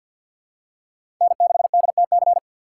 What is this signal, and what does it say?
Morse code, heard as a keyed tone, about 53 wpm, 700 Hz. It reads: N6DTC
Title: USA Call Signs - 6x Character Spacing - 50wpm